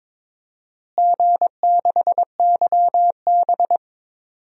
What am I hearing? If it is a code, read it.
Morse code, heard as a keyed tone, about 23 wpm, 700 Hz. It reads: G6YB
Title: DXCC-100 Call Signs - 4 Characters - Repeated 3x - 22wpm